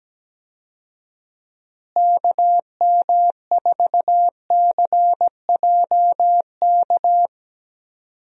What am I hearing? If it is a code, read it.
Morse code, heard as a keyed tone, about 17 wpm, 700 Hz. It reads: KM4CJK
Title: USA Call Signs - 6 Characters - Repeated 3x - 17wpm